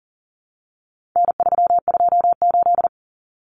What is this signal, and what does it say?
Morse code, heard as a keyed tone, about 40 wpm, 700 Hz. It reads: N328
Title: Common Cut Numbers - 4-Digit 40wpm